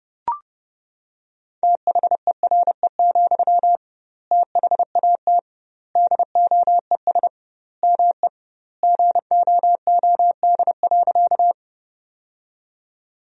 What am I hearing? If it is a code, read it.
Morse code, heard as a keyed tone, about 32 wpm, 700 Hz. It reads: THERE, THAT DOES ME GOOD.
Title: 2x Word-Spacing - Sentences from Top 200 Words - 30wpm